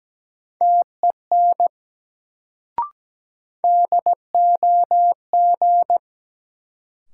Morse code, heard as a keyed tone, about 17 wpm, 700 Hz. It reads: TEN DOG
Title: Sentences from Top 400 Words - New Word Drill - 17wpm